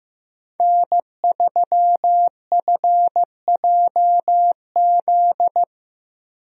Morse code, heard as a keyed tone, about 16 wpm, 700 Hz. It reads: N3FJZ
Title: USA Call Signs - 5 Characters - Repeated 2x - 15wpm